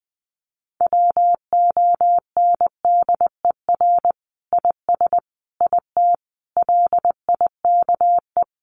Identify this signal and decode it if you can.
Morse code, heard as a keyed tone, about 20 wpm, 700 Hz. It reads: WONDER IS IT LIKE
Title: Sentences from Top 500 Words 20wpm